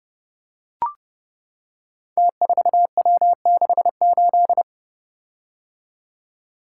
Morse code, heard as a keyed tone, about 31 wpm, 700 Hz. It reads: T4W68
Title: Code Groups - 5 Letter:Numbers - 30wpm